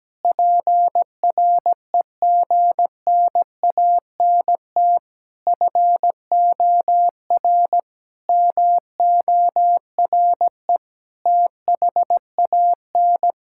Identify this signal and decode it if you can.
Morse code, heard as a keyed tone, about 17 wpm, 700 Hz. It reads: PREGNANT FOR MORE THAN
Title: Sentences from Top 1000 Words - Review Part 6 - 17wpm